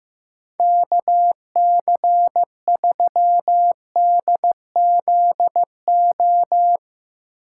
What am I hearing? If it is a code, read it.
Morse code, heard as a keyed tone, about 15 wpm, 700 Hz. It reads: KC3DZO
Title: USA Call Signs - 6 Characters - 6x Character Spacing - 15wpm